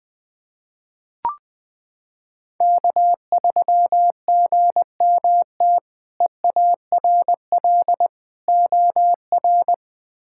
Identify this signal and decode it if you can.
Morse code, heard as a keyed tone, about 20 wpm, 700 Hz. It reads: K3GMT EARL OR
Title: K1USN SST QSO Practice - 20wpm